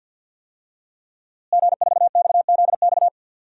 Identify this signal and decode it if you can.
Morse code, heard as a keyed tone, about 50 wpm, 700 Hz. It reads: G4XZX
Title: DXCC-100 Call Signs - 5 Characters - Repeated 3x - 50wpm